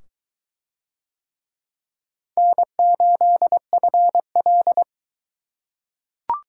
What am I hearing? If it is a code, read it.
Morse code, heard as a keyed tone, about 21 wpm, 700 Hz. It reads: N8FL